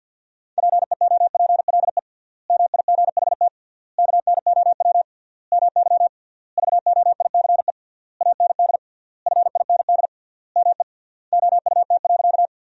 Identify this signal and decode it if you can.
Morse code, heard as a keyed tone, about 52 wpm, 700 Hz. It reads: PEOPLE MIGHT KNOW MY VOICE AND FIND ME OUT.
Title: 3x Word-Spacing - Sentences from Top 500 Words - 50wpm